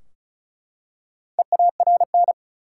Morse code, heard as a keyed tone, about 36 wpm, 700 Hz. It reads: EARN
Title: Four Letter Words - 35wpm